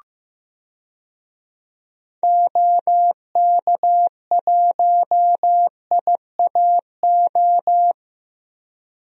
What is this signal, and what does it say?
Morse code, heard as a keyed tone, about 15 wpm, 700 Hz. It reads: OK1IAO
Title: DXCC-100 Call Signs - 6 Characters - Repeated 3x - 15wpm